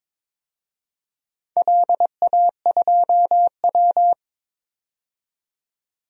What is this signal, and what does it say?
Morse code, heard as a keyed tone, about 22 wpm, 700 Hz. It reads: LA2W
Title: DXCC-100 Call Signs - 4 Characters - 22wpm